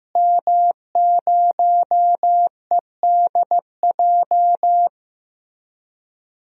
Morse code, heard as a keyed tone, about 15 wpm, 700 Hz. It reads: M0EDJ
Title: DXCC-100 Call Signs - 5 Characters - Repeated 3x - 15wpm